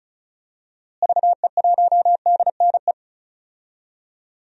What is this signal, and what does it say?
Morse code, heard as a keyed tone, about 34 wpm, 700 Hz. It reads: VE1DNE
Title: DXCC-100 Call Signs - 6 Characters - Repeated 2x - 35wpm